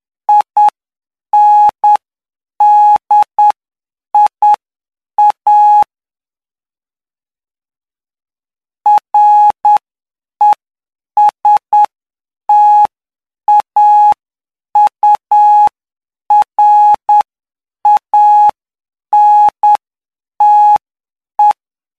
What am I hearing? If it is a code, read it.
Morse code, heard as a keyed tone, about 9 wpm, spaced as for 7 wpm, 840 Hz. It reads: INDIA RESTAURANTE
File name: cw / py2lcd / 05-PY2LCD-Li��o 14